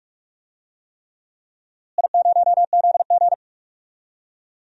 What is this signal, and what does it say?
Morse code, heard as a keyed tone, about 44 wpm, 700 Hz. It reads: I0ZG
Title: DXCC-100 Call Signs - 4 Characters - Repeated 3x - 45wpm